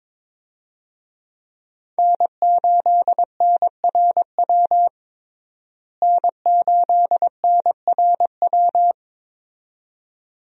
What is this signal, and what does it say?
Morse code, heard as a keyed tone, about 22 wpm, 700 Hz. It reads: N8NRW N8NRW
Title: USA Call Signs - 5 Characters - Repeated 2x - 22wpm